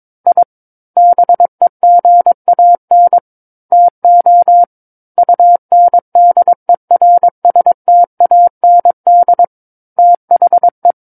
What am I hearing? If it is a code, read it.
Morse code, heard as a keyed tone, about 22 wpm, 700 Hz. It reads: I BEGAN TO UNDERSTAND THE
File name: Book One / 13-Chapter 13 - 22 WPM